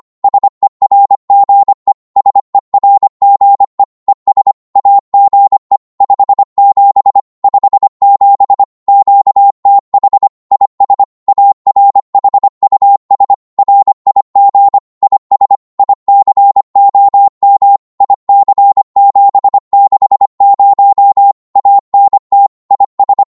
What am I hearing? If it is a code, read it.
Morse code, heard as a keyed tone, about 25 wpm, 800 Hz. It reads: SERGE SERGE ES AGE 57 57 QTH IS ARHUS RIG IS ICOM IC760 ANT IS
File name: Morse / qso-25wpm